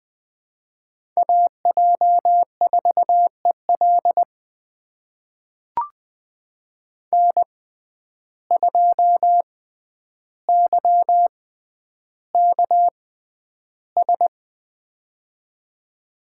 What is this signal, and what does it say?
Morse code, heard as a keyed tone, about 20 wpm, spaced as for 10 wpm, 700 Hz. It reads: AJ4EL N2YKS
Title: USA Call Signs - 5 Characters - 6x Character Spacing - 20wpm